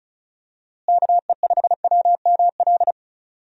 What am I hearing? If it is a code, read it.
Morse code, heard as a keyed tone, about 36 wpm, 700 Hz. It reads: KE5WML
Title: USA Call Signs - 6 Characters - Repeated 2x - 35wpm